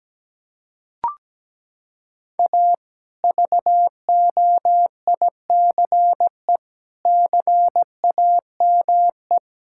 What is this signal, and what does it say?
Morse code, heard as a keyed tone, about 17 wpm, 700 Hz. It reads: A VOICE CAME
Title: Sentences from Top 1000 Words - Review Part 5 - 17wpm